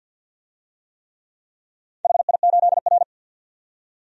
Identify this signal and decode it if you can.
Morse code, heard as a keyed tone, about 51 wpm, 700 Hz. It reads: HI8R
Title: DXCC-100 Call Signs - 4 Characters - 50wpm